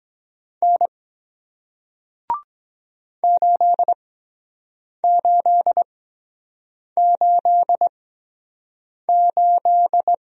Morse code, heard as a keyed tone, about 23 wpm, spaced as for 7 wpm, 700 Hz. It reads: N 8888